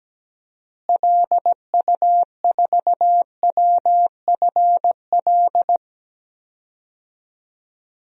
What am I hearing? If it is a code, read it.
Morse code, heard as a keyed tone, about 17 wpm, 700 Hz. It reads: LU4WFL
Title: DXCC-100 Call Signs - 6 Characters - Repeated 2x - 17wpm